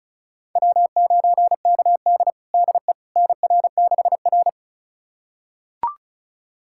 Morse code, heard as a keyed tone, about 36 wpm, 700 Hz. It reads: W9KD DE NR6R
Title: USA Callsign DE Callsign - 4 Characters - 35wpm